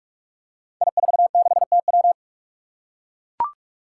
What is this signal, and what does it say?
Morse code, heard as a keyed tone, about 44 wpm, 700 Hz. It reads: I46TW
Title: Five Challenging Letter-Number Combinations 45wpm